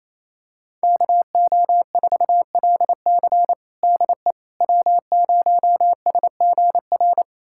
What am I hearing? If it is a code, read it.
Morse code, heard as a keyed tone, about 28 wpm, 690 Hz. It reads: KO4LC DE W0SGR
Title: USA Callsign DE Callsign - 5 Characters - 28wpm